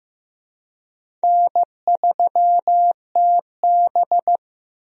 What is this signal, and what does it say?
Morse code, heard as a keyed tone, about 15 wpm, 700 Hz. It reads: N3TB
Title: USA Call Signs - 4 Characters - Repeated 3x - 15wpm